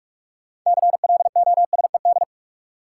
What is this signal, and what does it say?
Morse code, heard as a keyed tone, about 46 wpm, 700 Hz. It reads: CLOSED
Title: Sentences from Top 1400 Words - New Word Drill Alt - 45wpm